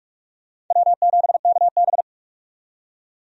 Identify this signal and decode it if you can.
Morse code, heard as a keyed tone, about 46 wpm, 700 Hz. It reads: W7KB
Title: USA Call Signs - 4 Characters - 45wpm